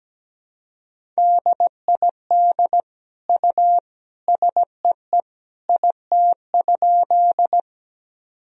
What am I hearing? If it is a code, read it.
Morse code, heard as a keyed tone, about 17 wpm, 700 Hz. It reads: DID U SEE IT?
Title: Sentences from Top 100 Words and Abbreviations 17wpm